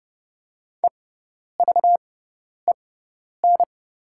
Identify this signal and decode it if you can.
Morse code, heard as a keyed tone, about 30 wpm, spaced as for 10 wpm, 700 Hz. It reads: EVEN
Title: 6x Character Spacing - Top 300 Words - 30wpm